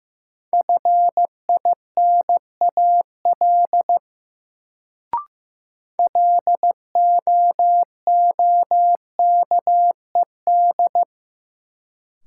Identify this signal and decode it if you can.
Morse code, heard as a keyed tone, about 15 wpm, 700 Hz. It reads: FINAL LOOKED
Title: Sentences from Top 800 Words - New Word Drill - 15wpm